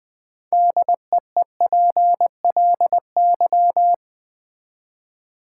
Morse code, heard as a keyed tone, about 20 wpm, 700 Hz. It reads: DEEPLY
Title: Build A Word - 3 Letters At A Time - 6-Letter Words - 20wpm